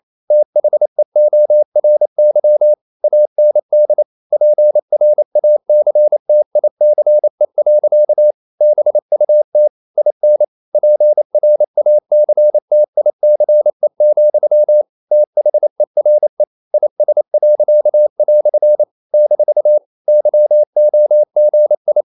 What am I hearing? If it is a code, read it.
Morse code, heard as a keyed tone, about 28 wpm, 600 Hz. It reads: THEORY AND PRACTICE. BUT IN PRACTICE, THERE IS." - YOGI
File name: CW / Practice / EnglishQuotations0128